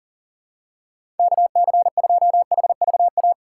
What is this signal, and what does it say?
Morse code, heard as a keyed tone, about 40 wpm, 700 Hz. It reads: KC2HVA
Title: USA Call Signs - 6x Character Spacing - 40wpm